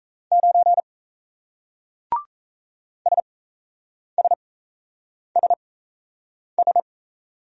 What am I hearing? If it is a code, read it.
Morse code, heard as a keyed tone, about 38 wpm, spaced as for 8 wpm, 700 Hz. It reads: 9 SSSS